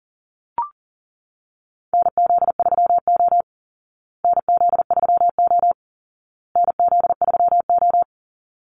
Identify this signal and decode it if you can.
Morse code, heard as a keyed tone, about 40 wpm, 700 Hz. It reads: NZ3O NZ3O NZ3O
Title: Call Signs Repeated 2x - 40wpm